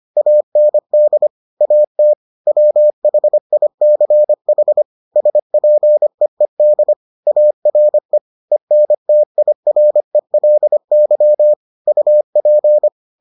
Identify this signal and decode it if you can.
Morse code, heard as a keyed tone, about 25 wpm, 600 Hz. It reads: AND AT WHICH SPEED ARE ENTIRELY UP